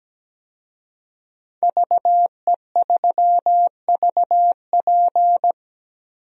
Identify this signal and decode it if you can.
Morse code, heard as a keyed tone, about 17 wpm, 700 Hz. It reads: VE3VP
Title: DXCC-100 Call Signs - 5 Characters - Repeated 2x - 17wpm